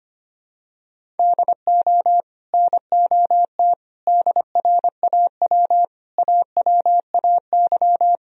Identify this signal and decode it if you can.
Morse code, heard as a keyed tone, about 25 wpm, 700 Hz. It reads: DO NOT DRAW AWAY
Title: Sentences from Top 1000 Words - Review Part 1 - 25wpm